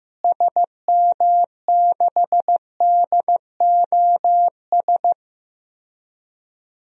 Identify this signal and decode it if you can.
Morse code, heard as a keyed tone, about 15 wpm, 700 Hz. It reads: SM6DOS
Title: DXCC-100 Call Signs - 6 Characters - Repeated 2x - 15wpm